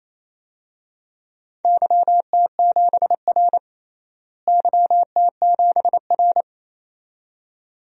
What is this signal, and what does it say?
Morse code, heard as a keyed tone, about 28 wpm, 700 Hz. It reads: YT7R YT7R
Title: DXCC-100 Call Signs - 4 Characters - Repeated 2x - 28wpm